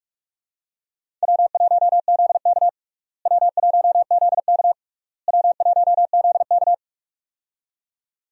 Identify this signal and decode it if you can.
Morse code, heard as a keyed tone, about 41 wpm, 700 Hz. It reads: W1ZK W1ZK W1ZK
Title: USA Call Signs - 4 Characters - Repeated 3x - 45wpm